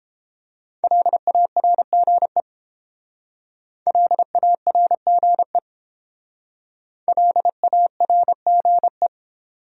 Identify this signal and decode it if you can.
Morse code, heard as a keyed tone, about 31 wpm, 700 Hz. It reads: LARGE LARGE LARGE